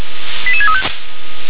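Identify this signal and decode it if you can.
Five-tone selective call (ZVEI), heard as quick runs of five tones, tones of about 70 ms. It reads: 88635
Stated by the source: ZVEI five-tone selective call received by SDR.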